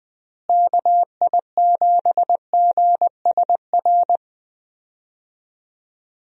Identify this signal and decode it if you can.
Morse code, heard as a keyed tone, about 20 wpm, 700 Hz. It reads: KI7GSR
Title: USA Call Signs - 6 Characters - Repeated 3x - 20wpm